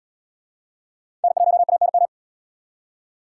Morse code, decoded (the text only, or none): N1ANM